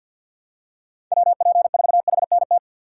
Morse code, heard as a keyed tone, about 52 wpm, 700 Hz. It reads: WP4HNT
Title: USA Call Signs - 6 Characters - 6x Character Spacing - 50wpm